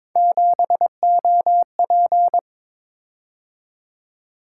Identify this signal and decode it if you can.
Morse code, heard as a keyed tone, about 22 wpm, 700 Hz. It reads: 7OP
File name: practice / 22wpm / Three Letter-Number Combinations - 22wpm